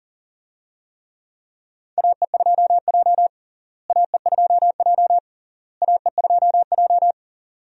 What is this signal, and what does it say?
Morse code, heard as a keyed tone, about 40 wpm, 700 Hz. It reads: AE2J AE2J AE2J
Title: USA Call Signs - 4 Characters - Repeated 3x - 40wpm